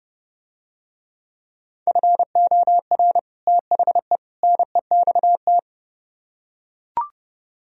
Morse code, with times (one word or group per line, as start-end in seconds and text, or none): 1.9-3.2 FOR
3.5-4.2 THE
4.4-5.6 NEXT